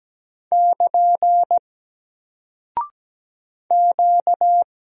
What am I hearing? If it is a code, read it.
Morse code, heard as a keyed tone, about 17 wpm, 700 Hz. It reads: <KN> Q
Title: Single Character (Letter:Number:Punctuation:Prosign) 17wpm